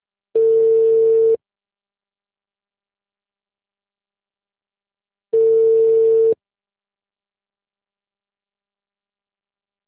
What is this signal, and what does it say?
Ringback tone (China, 450 Hz).